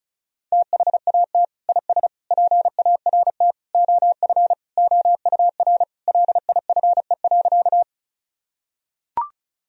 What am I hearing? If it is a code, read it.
Morse code, heard as a keyed tone, about 34 wpm, 700 Hz. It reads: THAT IS PART OF OUR LIFE.
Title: Sentences from Top 1000 Words - Review Part 1 - 35wpm